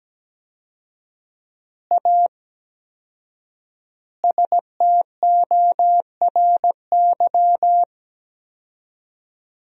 Morse code, heard as a keyed tone, about 17 wpm, 700 Hz. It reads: A STORY
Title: 4x Word-Spacing - Sentences from Top 500 Words - 17wpm